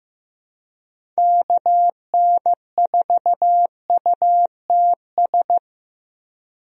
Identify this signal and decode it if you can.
Morse code, heard as a keyed tone, about 15 wpm, 700 Hz. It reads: KN4UTS
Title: USA Call Signs - 6 Characters - Repeated 3x - 15wpm